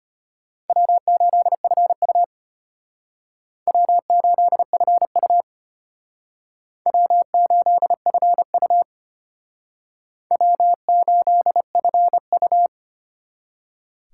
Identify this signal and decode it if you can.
Morse code, heard as a keyed tone, about 31 wpm, 700 Hz. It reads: W8FU W8FU W8FU W8FU